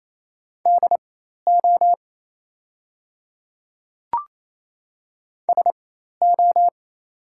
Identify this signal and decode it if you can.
Morse code, heard as a keyed tone, about 28 wpm, spaced as for 13 wpm, 700 Hz. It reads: DO SO
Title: 4x Character Spacing - Top 100 Words - 28wpm